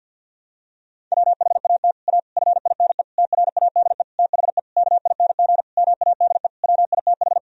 Morse code, heard as a keyed tone, about 52 wpm, 700 Hz. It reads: WHAT A FINE TRADE THE KING MADE WITH
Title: Sentences from Top 1000 Words - Review Part 4 - 50wpm